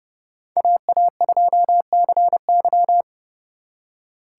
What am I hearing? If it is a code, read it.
Morse code, heard as a keyed tone, about 36 wpm, 700 Hz. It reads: AA2CY
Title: Speed Racing - USA 2x2 Call Signs 30wpm